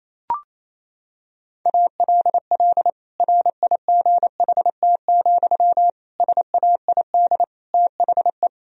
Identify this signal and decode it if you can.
Morse code, heard as a keyed tone, about 28 wpm, 700 Hz. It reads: ALL RIGHT, SAID THE
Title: Sentences from Top 1000 Words - Review Part 2 - 28wpm